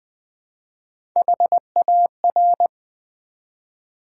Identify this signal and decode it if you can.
Morse code, heard as a keyed tone, about 20 wpm, 700 Hz. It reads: HAR
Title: Build A Word - 1 Letter At A Time - 5-Letter Words - 20wpm